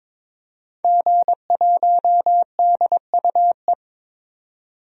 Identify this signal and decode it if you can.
Morse code, heard as a keyed tone, about 22 wpm, 700 Hz. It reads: G1DUE
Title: DXCC-100 Call Signs - 5 Characters - 22wpm